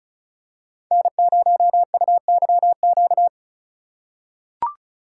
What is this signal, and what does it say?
Morse code, heard as a keyed tone, about 34 wpm, 700 Hz. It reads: N0UYQ
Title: USA Call Signs - 5 Characters - 4x Character Spacing - 35wpm